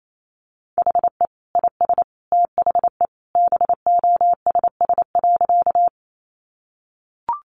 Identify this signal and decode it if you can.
Morse code, heard as a keyed tone, about 28 wpm, 700 Hz. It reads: HE IS THE BOSS.
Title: Sentences from Top 700 Words 28wpm